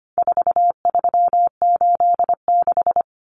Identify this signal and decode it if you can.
Morse code, heard as a keyed tone, about 25 wpm, 700 Hz. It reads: 4386
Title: Common Cut Numbers - 4-Digit 25wpm